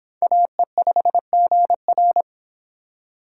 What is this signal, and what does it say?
Morse code, heard as a keyed tone, about 26 wpm, 700 Hz. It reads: AE5GR